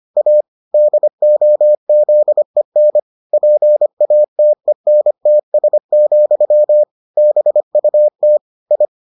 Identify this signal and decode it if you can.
Morse code, heard as a keyed tone, about 25 wpm, 600 Hz. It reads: A DOZEN PATENTS, BUT I